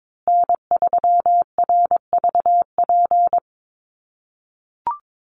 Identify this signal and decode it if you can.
Morse code, heard as a keyed tone, about 22 wpm, 700 Hz. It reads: N3RVP
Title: USA Call Signs Repeated 2x 22wpm